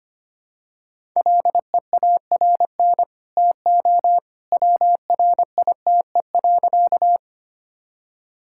Morse code, heard as a keyed tone, about 25 wpm, 700 Hz. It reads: LEARN TO WRITE.
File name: Easy Sentences with Hints - 25wpm